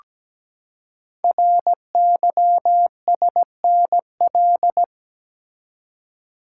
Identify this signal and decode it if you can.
Morse code, heard as a keyed tone, about 17 wpm, 700 Hz. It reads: RYSNL
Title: Code Groups - 5 Letters - 17wpm